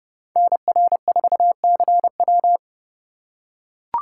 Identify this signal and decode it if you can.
Morse code, heard as a keyed tone, about 28 wpm, 700 Hz. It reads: NR4CW